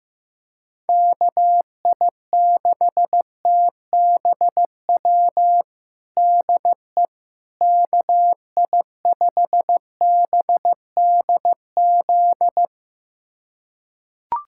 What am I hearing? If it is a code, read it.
Morse code, heard as a keyed tone, about 15 wpm, 700 Hz. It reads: KI6TBW DE KI5BDZ
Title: USA Callsign DE Callsign - 6 Characters - 15wpm